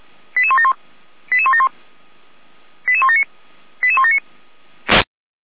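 Five-tone selective call (ZVEI), heard as quick runs of five tones, tones of about 70 ms. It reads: 88171 88171 88179 88179
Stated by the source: ZVEI five-tone selective call received on an SDR, from the feuerwehr-orlamuende fire brigade audio server.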